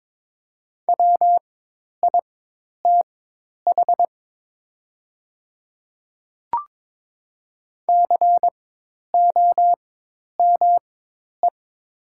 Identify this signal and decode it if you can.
Morse code, heard as a keyed tone, about 22 wpm, spaced as for 10 wpm, 700 Hz. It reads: WITH COME
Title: 4x Character Spacing - Top 100 Words - 22wpm